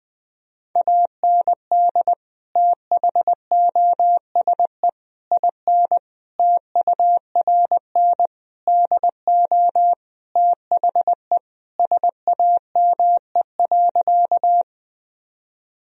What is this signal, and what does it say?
Morse code, heard as a keyed tone, about 20 wpm, 700 Hz. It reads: AND THOSE IN TURN DO THE SAME.
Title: Sentences from Top 1000 Words - Review Part 1 - 20wpm